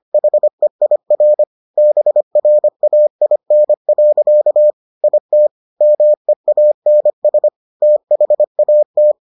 Morse code, heard as a keyed tone, about 25 wpm, 600 Hz. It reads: HEIR BRAIN. IT MEANS THAT